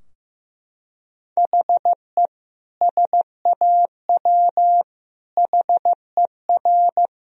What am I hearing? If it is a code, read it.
Morse code, heard as a keyed tone, about 16 wpm, 700 Hz. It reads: HE SAW HER
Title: Sentences from Top 1000 Words - Review Part 2 - 15wpm